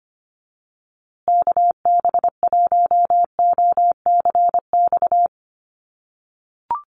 Morse code, heard as a keyed tone, about 25 wpm, 700 Hz. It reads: KB1OCX
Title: USA Call Signs Repeated 3x 25wpm